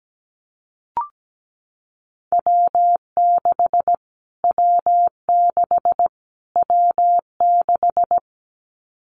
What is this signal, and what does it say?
Morse code, heard as a keyed tone, about 17 wpm, 700 Hz. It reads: W6 W6 W6
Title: Callsign Prefixes for Top 25 DXCC - Repeated 3x - 17wpm